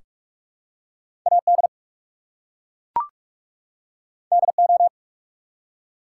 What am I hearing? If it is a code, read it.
Morse code, heard as a keyed tone, about 44 wpm, 700 Hz. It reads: AD DO
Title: Two Letter Words - 45wpm